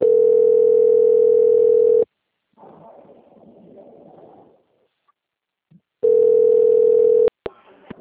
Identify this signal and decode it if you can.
Ringback tone (North America), 440 + 480 Hz.
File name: out- - - - - .25573